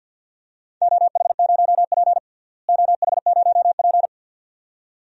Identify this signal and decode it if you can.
Morse code, heard as a keyed tone, about 52 wpm, 700 Hz. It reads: OH0P OH0P
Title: DXCC-100 Call Signs - 4 Characters - Repeated 2x - 50wpm